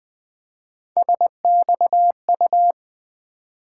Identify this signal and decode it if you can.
Morse code, heard as a keyed tone, about 20 wpm, 700 Hz. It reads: SXU
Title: Three Letter-Number Combinations - 20wpm